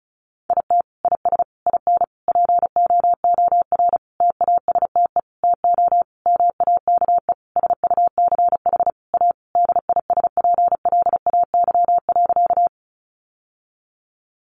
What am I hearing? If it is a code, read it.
Morse code, heard as a keyed tone, about 34 wpm, 700 Hz. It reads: IT IS IN POOR TASTE TO MAKE SUCH A DISPLAY.
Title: Sentences from Top 1200 Words 35wpm